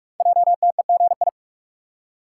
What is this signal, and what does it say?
Morse code, heard as a keyed tone, about 44 wpm, 690 Hz. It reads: JTEGI
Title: Code Groups - 5 Letters - 45wpm